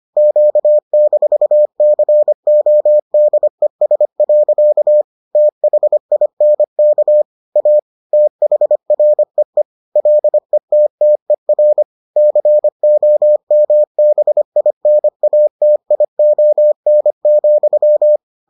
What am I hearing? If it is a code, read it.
Morse code, heard as a keyed tone, about 25 wpm, 600 Hz. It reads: Q-CODES. THINK A THREE LETTER COMBINATION,